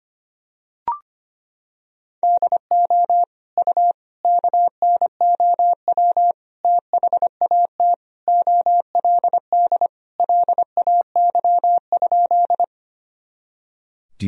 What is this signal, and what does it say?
Morse code, heard as a keyed tone, about 25 wpm, 700 Hz. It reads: DO U KNOW THAT OLD LAY?
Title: Sentences from Top 400 Words and Abbreviations 25wpm